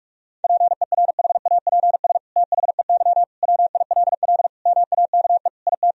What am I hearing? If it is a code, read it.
Morse code, heard as a keyed tone, about 41 wpm, 700 Hz. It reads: PERHAPS THEY WILL MAKE IT
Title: Sentences from Top 1000 Words - Review Part 2 - 45wpm